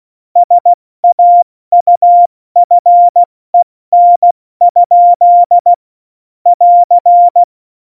Morse code, heard as a keyed tone, about 16 wpm, 700 Hz. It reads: SAUFEN? <AR>
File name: Morsekurs / Witze / Witz 04 August 2020, 70 F80